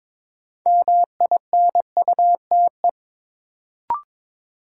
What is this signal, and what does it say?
Morse code, heard as a keyed tone, about 22 wpm, 700 Hz. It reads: MINUTE
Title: Sentences from Top 500 Words - New Word Drill - 22wpm